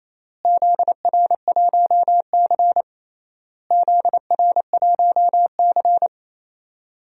Morse code, heard as a keyed tone, about 28 wpm, 700 Hz. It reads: ZR1C ZR1C
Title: DXCC-100 Call Signs - 4 Characters - Repeated 3x - 28wpm